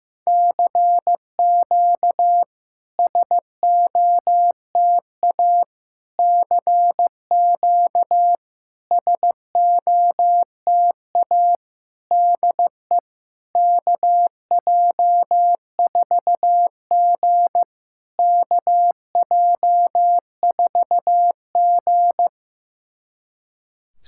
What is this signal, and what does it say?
Morse code, heard as a keyed tone, about 15 wpm, 700 Hz. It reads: CQ SOTA CQ SOTA DE KJ4G KJ4G
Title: SOTA QSO Practice 15wpm